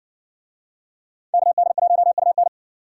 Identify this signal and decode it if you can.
Morse code, heard as a keyed tone, about 56 wpm, 700 Hz. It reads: KD1UN